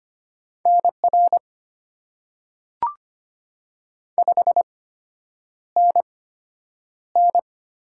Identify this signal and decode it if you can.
Morse code, heard as a keyed tone, about 25 wpm, spaced as for 9 wpm, 700 Hz. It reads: NR 5NN
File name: Top 90 QSO Elements - 8x Character Spacing - 25wpm